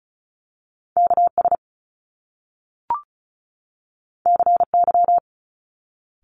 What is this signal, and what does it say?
Morse code, heard as a keyed tone, about 36 wpm, 700 Hz. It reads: KS CY
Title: Top Two Letter Combinations in Words 35wpm